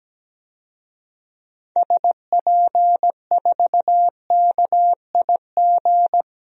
Morse code, heard as a keyed tone, about 17 wpm, 700 Hz. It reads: SP4KIG
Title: DXCC-100 Call Signs - 6 Characters - 17wpm